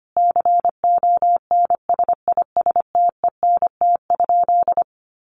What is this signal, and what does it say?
Morse code, heard as a keyed tone, about 25 wpm, 700 Hz. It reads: CONSISTENT?
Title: Sentences from Top 1500 Words 25wpm